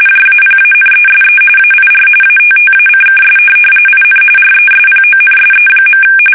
RTTY, about 45.45 baud. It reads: JAPANESE AIR ATTACKS IN RESPONSE.
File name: fsk850 / USS-Massachusetts